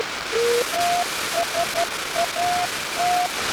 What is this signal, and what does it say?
Morse code, heard as a keyed tone, about 12 wpm, 700 Hz, with noise. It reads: TSAT
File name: Oscar 10 6 Aug 1983